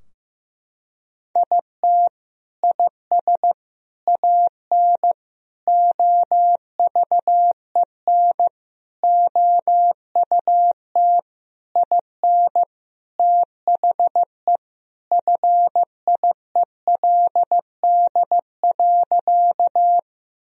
Morse code, heard as a keyed tone, about 15 wpm, 700 Hz. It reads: IT IS AN OVEN OUT IN THE FIELD.
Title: Sentences from Top 1000 Words - Review Part 4 - 15wpm